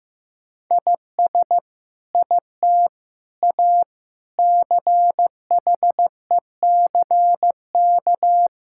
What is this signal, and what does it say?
Morse code, heard as a keyed tone, about 15 wpm, 700 Hz. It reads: IS IT A CHECK
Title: Sentences from Top 1000 Words - Review Part 6 - 15wpm